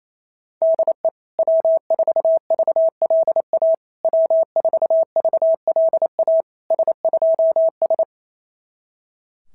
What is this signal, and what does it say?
Morse code, heard as a keyed tone, about 27 wpm, 660 Hz. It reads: DE W4VLA W4VLA S2S
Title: SOTA QSO Practice 28wpm